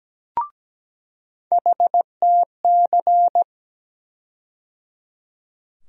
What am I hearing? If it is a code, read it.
Morse code, heard as a keyed tone, about 17 wpm, 700 Hz. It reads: HTC
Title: USA Call Sign Postfixes - 3 Letters 17wpm